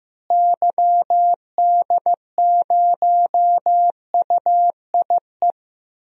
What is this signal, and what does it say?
Morse code, heard as a keyed tone, about 15 wpm, 700 Hz. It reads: YD0UIE
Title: DXCC-100 Call Signs - 6 Characters - Repeated 2x - 15wpm